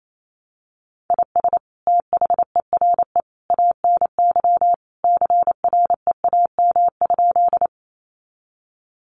Morse code, heard as a keyed tone, about 28 wpm, 700 Hz. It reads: IS THERE ANY CREAM?
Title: Sentences from Top 1500 Words 28wpm